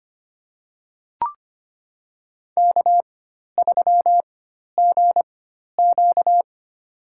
Morse code, heard as a keyed tone, about 25 wpm, spaced as for 12 wpm, 700 Hz. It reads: K3GQ
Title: USA Call Signs - 4 Characters - 4x Character Spacing - 25wpm